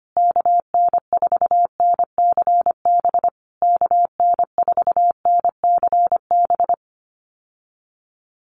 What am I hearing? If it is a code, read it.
Morse code, heard as a keyed tone, about 25 wpm, 700 Hz. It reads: KN4NCB KN4NCB
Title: USA Call Signs Repeated 3x 25wpm